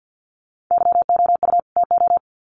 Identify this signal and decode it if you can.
Morse code, heard as a keyed tone, about 50 wpm, 700 Hz. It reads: YOU TO
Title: Sets of 2 Words 50wpm